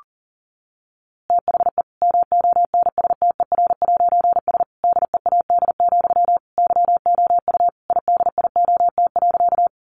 Morse code, heard as a keyed tone, about 40 wpm, 700 Hz. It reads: THE MONSTER'S DEAD, YOU IDIOT.
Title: A Sound Of Thunder 40wpm